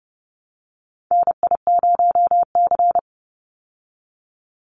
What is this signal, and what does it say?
Morse code, heard as a keyed tone, about 30 wpm, 700 Hz. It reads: NI0C
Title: Call Signs Repeated 2x - 30wpm